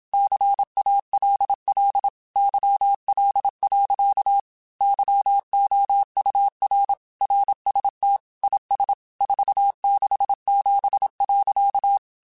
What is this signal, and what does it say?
Morse code, heard as a keyed tone, about 27 wpm, 800 Hz. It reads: CALL YL. YOUR RST IS 467.
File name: QSOs / QSOs-mp3s / WA7QOX 2@25